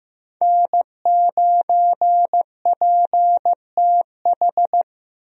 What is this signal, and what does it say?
Morse code, heard as a keyed tone, about 15 wpm, 700 Hz. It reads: N9PTH
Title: USA Call Signs - 5 Characters - Repeated 2x - 15wpm